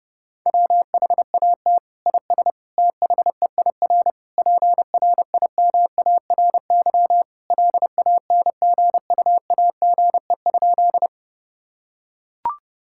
Morse code, heard as a keyed tone, about 30 wpm, 700 Hz. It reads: WHAT IS THEIR PRIMARY LANGUAGE?
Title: Sentences from Top 1000 Words - Review Part 5 - 30wpm